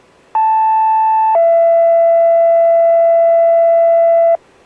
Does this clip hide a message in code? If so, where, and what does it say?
Yes: two-tone page from 0.3 to 4.4 s, heard as a one-second tone, then a three-second tone, about 880 then 670 Hz.